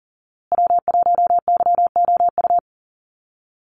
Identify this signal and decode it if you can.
Morse code, heard as a keyed tone, about 40 wpm, 700 Hz. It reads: W1YOU
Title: USA Call Signs Repeated 3x 40wpm